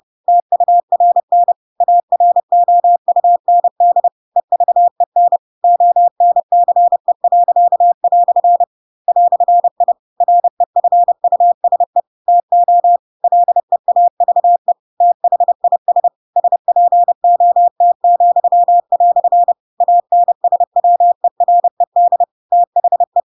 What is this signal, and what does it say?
Morse code, heard as a keyed tone, about 30 wpm, 700 Hz. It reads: TURN AROUND EVEN ONCE." "I REFUSE TO LEAVE THIS SPOT," ANSWERED THE
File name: Pinocchio / Pinocchio ch4 30 30 ik1hge